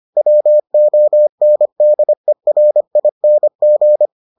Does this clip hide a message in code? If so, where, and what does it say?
Yes: Morse code from 0.2 to 4.1 s, heard as a keyed tone, about 25 wpm, 600 Hz. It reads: WONDERING